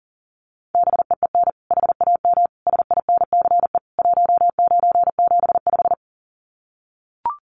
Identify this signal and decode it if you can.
Morse code, heard as a keyed tone, about 40 wpm, 700 Hz. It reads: BEEN HAM SINCE 1975
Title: Common QSO Phrases 40wpm